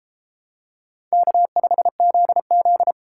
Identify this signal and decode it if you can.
Morse code, heard as a keyed tone, about 31 wpm, 700 Hz. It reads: K5ZZ